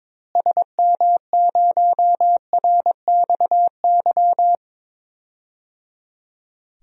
Morse code, heard as a keyed tone, about 21 wpm, 700 Hz. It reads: SM0RXY
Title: DXCC-100 Call Signs - 6 Characters - Repeated 3x - 22wpm